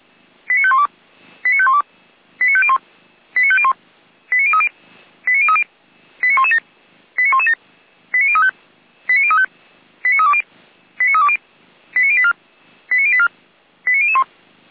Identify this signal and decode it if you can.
Five-tone selective call (ZVEI), heard as quick runs of five tones, tones of about 80 ms. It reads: 89512 89512 89501 89501 89030 89030 89117 89117 89035 89035 89320 89320 89084 89084 89001